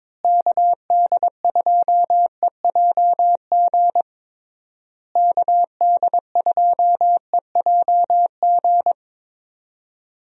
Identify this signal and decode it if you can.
Morse code, heard as a keyed tone, about 22 wpm, 700 Hz. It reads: KD2EJG KD2EJG
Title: USA Call Signs - 6 Characters - Repeated 3x - 22wpm